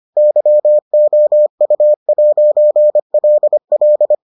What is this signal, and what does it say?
Morse code, heard as a keyed tone, about 25 wpm, 600 Hz. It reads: YOU'LL